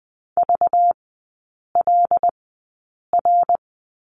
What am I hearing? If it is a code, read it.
Morse code, heard as a keyed tone, about 20 wpm, spaced as for 8 wpm, 700 Hz. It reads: VLR
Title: 191 Lesson 31e - Letter v - Three Characters 20wpm